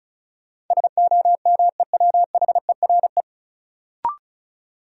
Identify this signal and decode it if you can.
Morse code, heard as a keyed tone, about 36 wpm, 700 Hz. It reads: SOMEWHERE
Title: Sentences from Top 1400 Words - New Word Drill - 35wpm